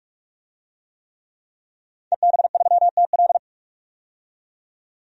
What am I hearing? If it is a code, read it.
Morse code, heard as a keyed tone, about 42 wpm, 700 Hz. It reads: EB3TL
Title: DXCC-100 Call Signs - 5 Characters - 45wpm